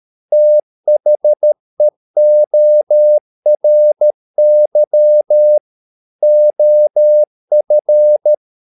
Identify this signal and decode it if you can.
Morse code, heard as a keyed tone, about 13 wpm, 600 Hz. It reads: THEORY OF